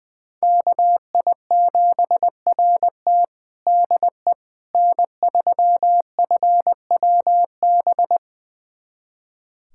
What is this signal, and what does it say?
Morse code, heard as a keyed tone, about 20 wpm, 700 Hz. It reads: KI7RT DE N3FWB
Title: USA Callsign DE Callsign - 5 Characters - 20wpm